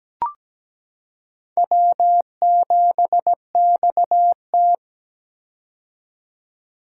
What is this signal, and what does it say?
Morse code, heard as a keyed tone, about 17 wpm, 700 Hz. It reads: W7XT
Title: K1USN SST QSO Practice - 17wpm